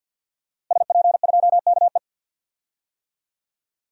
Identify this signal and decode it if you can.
Morse code, heard as a keyed tone, about 50 wpm, 700 Hz. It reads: SP2KE
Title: DXCC-100 Call Signs - 5 Characters - Repeated 3x - 50wpm